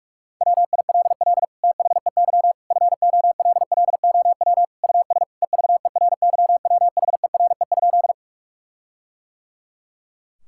Morse code, heard as a keyed tone, about 46 wpm, 700 Hz. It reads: WILL THEY FOLLOW US EVERYWHERE?
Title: Sentences from Top 1600 Words 45wpm